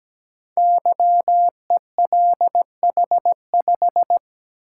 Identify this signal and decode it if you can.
Morse code, heard as a keyed tone, about 17 wpm, 700 Hz. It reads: YELH5
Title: Code Groups - 5 Letter:Numbers - 17wpm